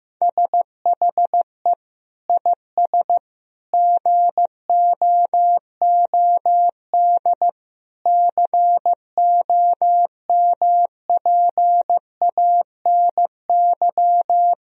Morse code, heard as a keyed tone, about 15 wpm, 700 Hz. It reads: SHE IS GOOD COMPANY
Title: Sentences from Top 1000 Words - Review Part 3 - 15wpm